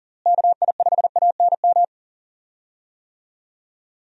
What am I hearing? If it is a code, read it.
Morse code, heard as a keyed tone, about 40 wpm, 700 Hz. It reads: KI5ANM
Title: USA Call Signs - 6 Characters - Repeated 3x - 40wpm